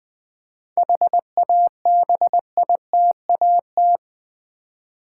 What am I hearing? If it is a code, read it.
Morse code, heard as a keyed tone, about 20 wpm, 700 Hz. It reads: HABITAT
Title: 7 Letter Words - First 3 Letters Given - 20wpm